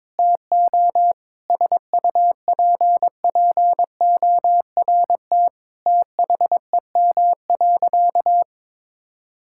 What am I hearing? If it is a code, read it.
Morse code, heard as a keyed tone, about 22 wpm, 700 Hz. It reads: TO SUPPORT THEM.
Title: Sentences from Top 1000 Words - Review Part 5 - 22wpm